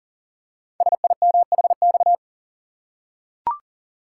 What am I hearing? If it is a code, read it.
Morse code, heard as a keyed tone, about 40 wpm, 700 Hz. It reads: SIMHX
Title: Code Groups - 5 Letters - 40wpm